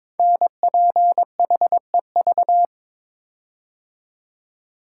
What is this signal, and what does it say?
Morse code, heard as a keyed tone, about 23 wpm, 700 Hz. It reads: NPHEV
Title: Code Groups - 5 Letters - 22wpm